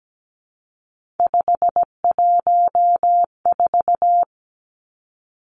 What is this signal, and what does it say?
Morse code, heard as a keyed tone, about 17 wpm, 700 Hz. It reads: 514
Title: Three Challenging Letter-Number Combinations 17wpm